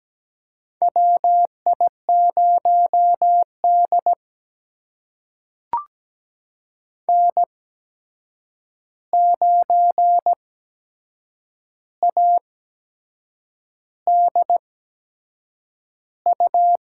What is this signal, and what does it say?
Morse code, heard as a keyed tone, about 17 wpm, spaced as for 7 wpm, 700 Hz. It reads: WI0D N9ADU
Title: USA Call Signs - 8x Character Spacing - 17wpm